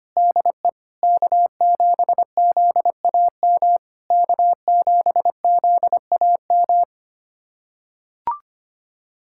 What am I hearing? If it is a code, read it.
Morse code, heard as a keyed tone, about 25 wpm, 700 Hz. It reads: DE K7ZAM K7ZAM
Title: SOTA QSO Practice 25wpm